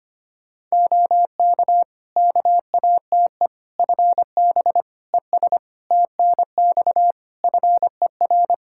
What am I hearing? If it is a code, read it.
Morse code, heard as a keyed tone, about 25 wpm, 700 Hz. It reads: OK KATE FB ES TNX FER
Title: Practice QSO - Exchange 2 - Info - 25wpm